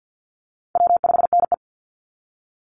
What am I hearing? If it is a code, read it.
Morse code, heard as a keyed tone, about 50 wpm, 700 Hz. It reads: W5NE
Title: Easy Call Signs - 50wpm